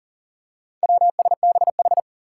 Morse code, heard as a keyed tone, about 41 wpm, 700 Hz. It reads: WSBH
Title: Four Challenging Letter-Number Combinations 40wpm